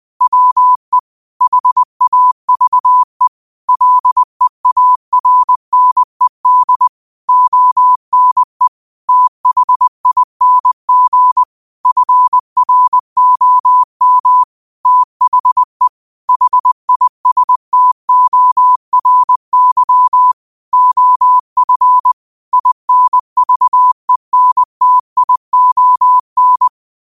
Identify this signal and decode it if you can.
Morse code, heard as a keyed tone, about 20 wpm, 1000 Hz. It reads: WE HAVE LEARNED ONE THING FROM THE HISTORY OF INVENTION
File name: qotdfiles / 20wpm